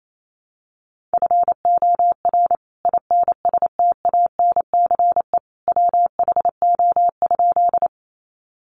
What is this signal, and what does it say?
Morse code, heard as a keyed tone, about 28 wpm, 700 Hz. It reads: FOR INSTANCE WHO?
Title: Sentences from Top 800 Words 28wpm